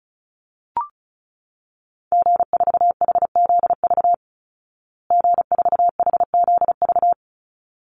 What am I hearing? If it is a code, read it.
Morse code, heard as a keyed tone, about 34 wpm, 700 Hz. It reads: G4HZV G4HZV
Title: Call Signs Repeated 3x - 35wpm